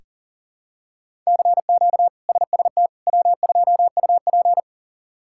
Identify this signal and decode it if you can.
Morse code, heard as a keyed tone, about 40 wpm, 700 Hz. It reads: CQ SST W2UP
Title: K1USN SST QSO Practice - 40wpm